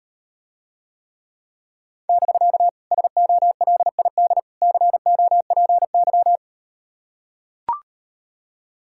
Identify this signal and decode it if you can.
Morse code, heard as a keyed tone, about 38 wpm, 700 Hz. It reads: <BK> SOLID COPY